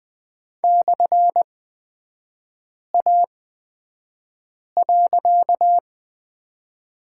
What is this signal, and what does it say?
Morse code, heard as a keyed tone, about 20 wpm, spaced as for 5 wpm, 700 Hz. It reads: /A.
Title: 255 Lesson 41e - Period - Three Characters 20wpm